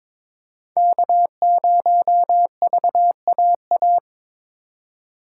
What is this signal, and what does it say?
Morse code, heard as a keyed tone, about 28 wpm, 700 Hz. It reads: K0VAA